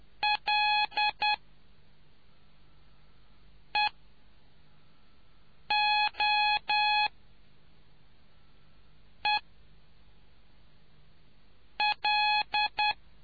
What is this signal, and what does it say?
Morse code, heard as a keyed tone, about 10 wpm, spaced as for 4 wpm, 840 Hz. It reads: L EO E L